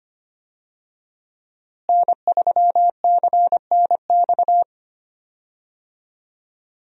Morse code, heard as a keyed tone, about 25 wpm, 700 Hz. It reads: N3CNX
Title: USA Call Signs - 5 Characters - Repeated 2x - 25wpm